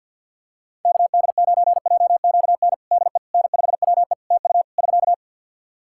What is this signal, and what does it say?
Morse code, heard as a keyed tone, about 50 wpm, 700 Hz. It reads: KD9JQN DE N5WE TU <SK>
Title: Practice QSO - Exchange 4 - Ending - 50wpm